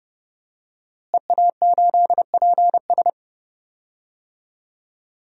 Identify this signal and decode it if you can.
Morse code, heard as a keyed tone, about 30 wpm, 700 Hz. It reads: EA8PS
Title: DXCC-100 Call Signs - 5 Characters - Repeated 3x - 30wpm